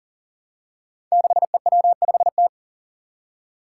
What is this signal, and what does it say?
Morse code, heard as a keyed tone, about 40 wpm, 700 Hz. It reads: 6EW5T
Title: Five Challenging Letter-Number Combinations 40wpm